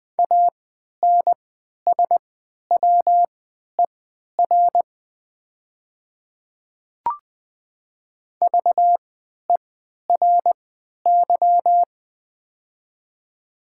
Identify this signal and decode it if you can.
Morse code, heard as a keyed tone, about 20 wpm, spaced as for 11 wpm, 700 Hz. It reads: ANSWER VERY
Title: 3x Character Spacing - Top 500 Words - 20wpm